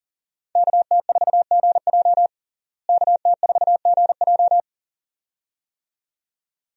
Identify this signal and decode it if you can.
Morse code, heard as a keyed tone, about 40 wpm, 700 Hz. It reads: KT4GJ KT4GJ
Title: USA Call Signs - 5 Characters - Repeated 2x - 40wpm